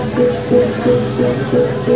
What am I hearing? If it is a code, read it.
Speech. There is no code.